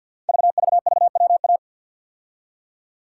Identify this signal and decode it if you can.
Morse code, heard as a keyed tone, about 51 wpm, 700 Hz. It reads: VVVWA
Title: Code Groups - 5 Letters - 50wpm